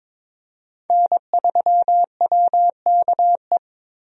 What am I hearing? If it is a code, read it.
Morse code, heard as a keyed tone, about 22 wpm, 700 Hz. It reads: N3WKE
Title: USA Call Signs - 5 Characters - Repeated 2x - 22wpm